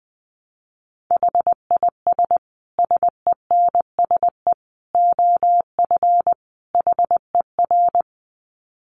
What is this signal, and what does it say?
Morse code, heard as a keyed tone, about 20 wpm, 700 Hz. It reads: HIS SENSE OF HER
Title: Dit Challenge 20wpm